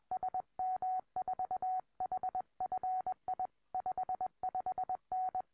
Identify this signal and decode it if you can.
Morse code, heard as a keyed tone, about 21 wpm, 740 Hz. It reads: SM4HFI 55N
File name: tropo / sm4hfi